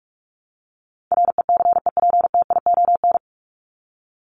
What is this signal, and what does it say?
Morse code, heard as a keyed tone, about 44 wpm, 700 Hz. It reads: RECEPTION
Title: Top 1751-2000 Words 45wpm